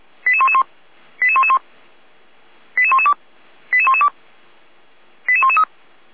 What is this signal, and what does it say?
Five-tone selective call (ZVEI), heard as quick runs of five tones, tones of about 70 ms. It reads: 88111 88111 88112 88112 88113